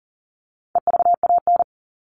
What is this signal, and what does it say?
Morse code, heard as a keyed tone, about 40 wpm, 700 Hz. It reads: EVAN
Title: All Cut Numbers - 4-Digit 40wpm